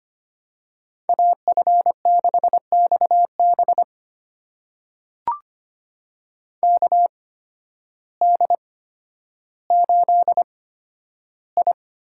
Morse code, heard as a keyed tone, about 25 wpm, spaced as for 12 wpm, 700 Hz. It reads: AF6XB KD8I
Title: USA Call Signs - 5 Characters - 8x Character Spacing - 25wpm